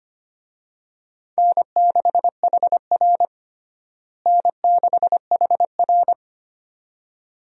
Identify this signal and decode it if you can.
Morse code, heard as a keyed tone, about 25 wpm, 700 Hz. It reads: N6HR N6HR
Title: USA Call Signs - 4 Characters - Repeated 2x - 25wpm